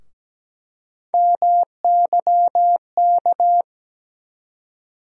Morse code, heard as a keyed tone, about 17 wpm, 700 Hz. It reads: MYK
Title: USA Call Sign Postfixes - 3 Letters 17wpm